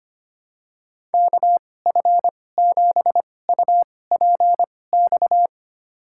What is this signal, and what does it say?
Morse code, heard as a keyed tone, about 25 wpm, spaced as for 18 wpm, 700 Hz. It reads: KF7UPX
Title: USA Call Signs - 6 Characters - 2x Character Spacing - 25wpm